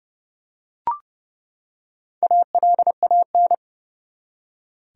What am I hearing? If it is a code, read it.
Morse code, heard as a keyed tone, about 30 wpm, 700 Hz. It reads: ALAN
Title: USA Common Names - Max 4 Letters - 30wpm